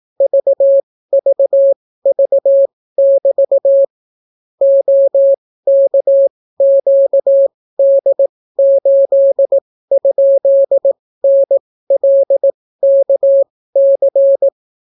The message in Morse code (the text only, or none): VVV<BT> OKQD8?NLKC